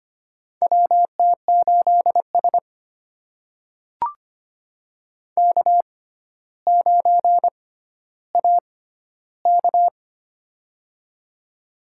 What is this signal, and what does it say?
Morse code, heard as a keyed tone, about 25 wpm, spaced as for 13 wpm, 700 Hz. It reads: WT8S K9AK
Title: USA Call Signs - 4 Characters - 6x Character Spacing - 25wpm